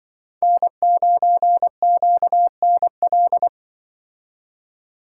Morse code, heard as a keyed tone, about 24 wpm, 700 Hz. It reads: N9QNL
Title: Speed Racing - USA Call Signs 20wpm